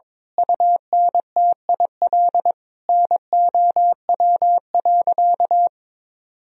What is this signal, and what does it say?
Morse code, heard as a keyed tone, about 22 wpm, 700 Hz. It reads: UNTIL NOW.
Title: Sentences from Top 1000 Words - Review Part 3 - 22wpm